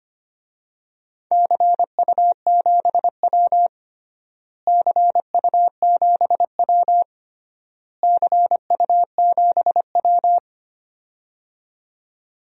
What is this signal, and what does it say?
Morse code, heard as a keyed tone, about 25 wpm, 700 Hz. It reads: CU7W CU7W CU7W
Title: DXCC-100 Call Signs - 4 Characters - Repeated 3x - 25wpm